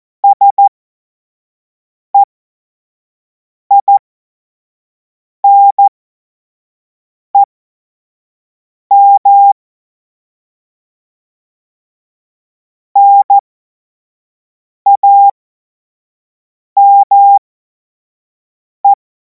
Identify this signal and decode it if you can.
Morse code, heard as a keyed tone, about 14 wpm, spaced as for 5 wpm, 800 Hz. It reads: SEINEM NAME
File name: Morsekurs / Witze / Witz 02. Juni 2026 25 F70